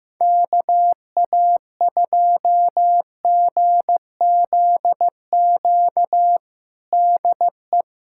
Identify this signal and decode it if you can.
Morse code, heard as a keyed tone, about 15 wpm, 700 Hz. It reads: KA2GZQ DE
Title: USA Callsign DE Callsign 15wpm